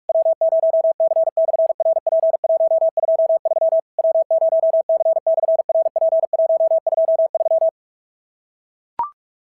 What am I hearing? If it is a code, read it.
Morse code, heard as a keyed tone, about 44 wpm, 690 Hz. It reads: W0C/RP123 W0C/RP123
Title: SOTA QSO Practice 45wpm